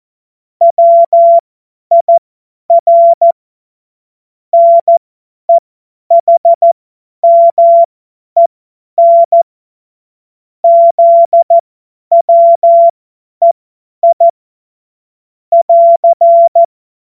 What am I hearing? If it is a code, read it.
Morse code, heard as a keyed tone, about 14 wpm, spaced as for 10 wpm, 680 Hz. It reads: WIR NEHMEN ZWEI <AR>
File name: Morsekurs / Witze / Witz 11. Januar 2022, 50 F70